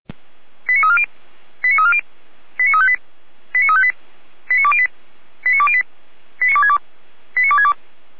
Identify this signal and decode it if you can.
Five-tone selective call (ZVEI), heard as quick runs of five tones, tones of about 80 ms. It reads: 89340 89340 89358 89358 89208 89208 89262 89262